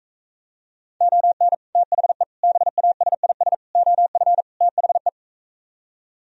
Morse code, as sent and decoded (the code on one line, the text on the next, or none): --- -. / - .... / -... .- ... .. ... / --- ..-. / - .... .
ON THE BASIS OF THE